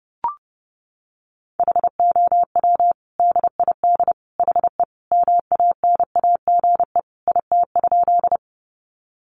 Morse code, heard as a keyed tone, about 30 wpm, 700 Hz. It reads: HOW DID HE MANAGE IT?
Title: Sentences from Top 1400 Words 30wpm